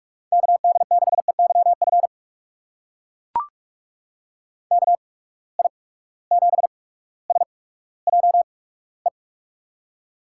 Morse code, heard as a keyed tone, about 44 wpm, spaced as for 19 wpm, 700 Hz. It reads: KD6EYF KI7SJE
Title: USA Call Signs - 6 Characters - 8x Character Spacing - 45wpm